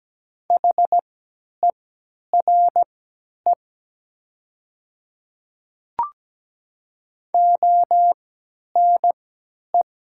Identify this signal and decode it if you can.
Morse code, heard as a keyed tone, about 17 wpm, spaced as for 10 wpm, 700 Hz. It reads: HERE ONE
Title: 3x Character Spacing - Top 300 Words - 17wpm